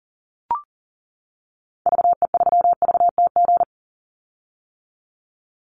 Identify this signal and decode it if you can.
Morse code, heard as a keyed tone, about 40 wpm, 700 Hz. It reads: VE3VTG
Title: Call Signs - 40wpm